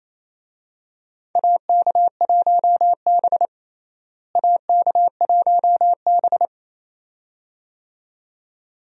Morse code, heard as a keyed tone, about 28 wpm, 700 Hz. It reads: AK1B AK1B
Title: USA Call Signs - 4 Characters - Repeated 2x - 28wpm